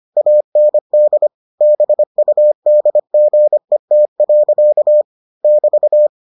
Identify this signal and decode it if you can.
Morse code, heard as a keyed tone, about 25 wpm, 600 Hz. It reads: AND BUDGET. <BT>